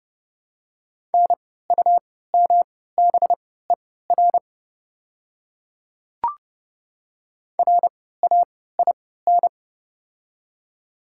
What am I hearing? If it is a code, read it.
Morse code, heard as a keyed tone, about 30 wpm, spaced as for 17 wpm, 700 Hz. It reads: NUMBER RAIN
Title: 3x Character Spacing - Top 300 Words - 30wpm